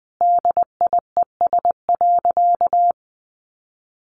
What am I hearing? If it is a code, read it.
Morse code, heard as a keyed tone, about 20 wpm, 700 Hz. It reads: DIES.
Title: A Sound Of Thunder 20wpm